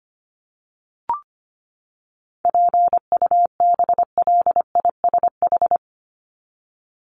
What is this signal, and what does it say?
Morse code, heard as a keyed tone, about 25 wpm, 700 Hz. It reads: PUBLISH
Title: ED Suffix - Top 100 Words with Base Words 25wpm